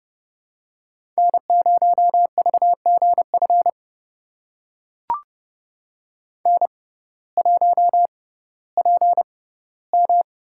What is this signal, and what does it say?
Morse code, heard as a keyed tone, about 30 wpm, spaced as for 16 wpm, 700 Hz. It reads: N0VGF N1PM
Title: USA Call Signs - 5 Characters - 6x Character Spacing - 30wpm